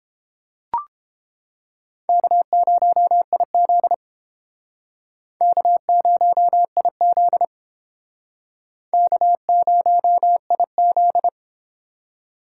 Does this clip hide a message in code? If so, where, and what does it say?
Yes: Morse code from 2.1 to 11.3 s, heard as a keyed tone, about 32 wpm, 700 Hz. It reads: K0IZ K0IZ K0IZ